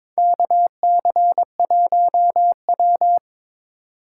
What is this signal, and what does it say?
Morse code, heard as a keyed tone, about 23 wpm, 700 Hz. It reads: KC1W
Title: USA Call Signs - 4 Characters - 8x Character Spacing - 22wpm